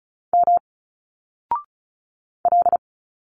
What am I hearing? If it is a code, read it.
Morse code, heard as a keyed tone, about 36 wpm, 700 Hz. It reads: M L
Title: Single Letters 35wpm